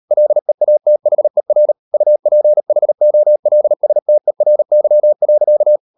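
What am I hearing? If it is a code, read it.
Morse code, heard as a keyed tone, about 38 wpm, 600 Hz. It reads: LEATHER UPHOLSTERY.
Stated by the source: Morse code practice at 38 WPM, sending Ken Follett's novel Never in English, from CW med Gnister.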